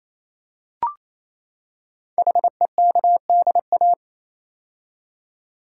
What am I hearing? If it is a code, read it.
Morse code, heard as a keyed tone, about 27 wpm, 700 Hz. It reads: HEKDA
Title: Code Groups - 5 Letters - 28wpm